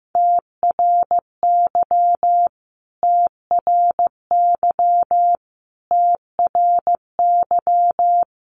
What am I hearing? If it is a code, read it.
Morse code, heard as a keyed tone, about 15 wpm, 700 Hz. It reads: TRY TRY TRY
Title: Top 500 Words Repeated 15wpm